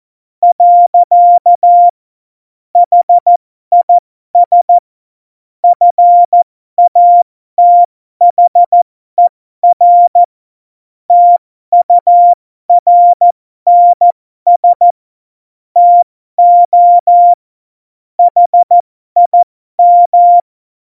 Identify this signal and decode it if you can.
Morse code, heard as a keyed tone, about 14 wpm, spaced as for 12 wpm, 700 Hz. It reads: . HIS FATHER TURNS TO HIM
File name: Morsekurs / Witze